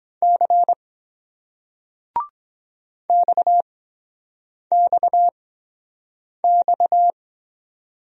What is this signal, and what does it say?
Morse code, heard as a keyed tone, about 24 wpm, spaced as for 7 wpm, 700 Hz. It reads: C XXX